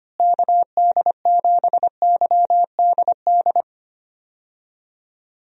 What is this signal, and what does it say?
Morse code, heard as a keyed tone, about 25 wpm, 700 Hz. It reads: KD7YDD
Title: USA Call Signs - 6 Characters - Repeated 2x - 25wpm